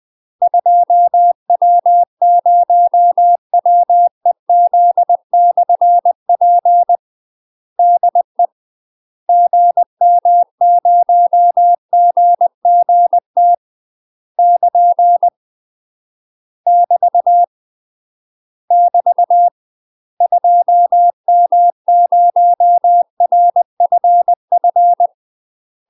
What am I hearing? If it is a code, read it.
Morse code, heard as a keyed tone, about 20 wpm, 700 Hz. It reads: 2W0WEZ/P DE GM0GGT <KN> <BT> <BT> 2M0RFF
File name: UK MASTER QSOs TRAINING / 20WPM / UK MASTER QSOs 0048